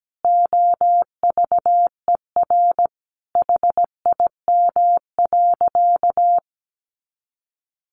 Morse code, heard as a keyed tone, about 17 wpm, 700 Hz. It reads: OVER HIM.
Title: Additional Easy Phrases 17wpm